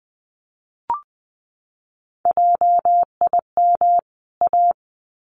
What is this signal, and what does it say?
Morse code, heard as a keyed tone, about 20 wpm, 700 Hz. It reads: JIM A
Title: CWT Exchange 20wpm